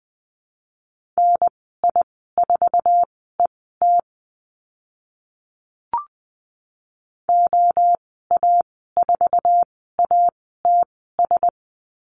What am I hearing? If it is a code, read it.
Morse code, heard as a keyed tone, about 20 wpm, spaced as for 14 wpm, 700 Hz. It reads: NI4ET OA4ATS
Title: 088 Lesson 15g - Number 5 - Call Signs 20wpm